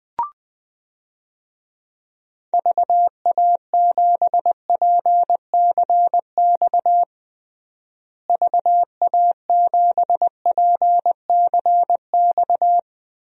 Morse code, heard as a keyed tone, about 21 wpm, 700 Hz. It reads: VA7PCX VA7PCX
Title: DXCC-100 Call Signs - 6 Characters - Repeated 3x - 20wpm